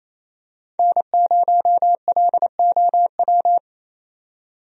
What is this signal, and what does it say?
Morse code, heard as a keyed tone, about 28 wpm, 700 Hz. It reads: N0LOW
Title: USA Call Signs - 5 Characters - 28wpm